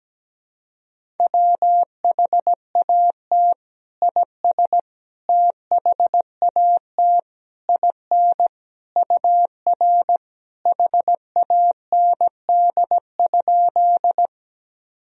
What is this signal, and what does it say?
Morse code, heard as a keyed tone, about 17 wpm, 700 Hz. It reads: WHAT IS THAT IN UR HAND?
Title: Sentences from Top 200 Words and Abbreviations 17wpm